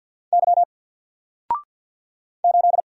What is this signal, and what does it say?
Morse code, heard as a keyed tone, about 50 wpm, 700 Hz. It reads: Y 8
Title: Single Character (Letter:Number:Punctuation:Prosign) 50wpm